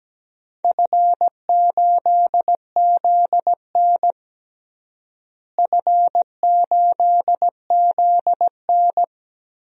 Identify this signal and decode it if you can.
Morse code, heard as a keyed tone, about 17 wpm, 700 Hz. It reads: F8ZN F8ZN
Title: DXCC-100 Call Signs - 4 Characters - Repeated 3x - 17wpm